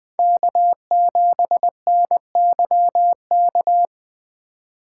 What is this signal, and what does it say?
Morse code, heard as a keyed tone, about 20 wpm, 700 Hz. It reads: K7NYK
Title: USA Call Signs - 5 Characters - 4x Character Spacing - 20wpm